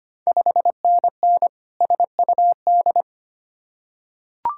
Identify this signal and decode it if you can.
Morse code, heard as a keyed tone, about 25 wpm, 700 Hz. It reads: 5NN SUD
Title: Ontario QSO Party Exchange 25wpm